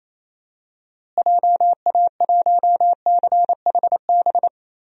Morse code, heard as a keyed tone, about 28 wpm, 700 Hz. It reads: JA1CHB
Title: DXCC-100 Call Signs - 6 Characters - Repeated 3x - 28wpm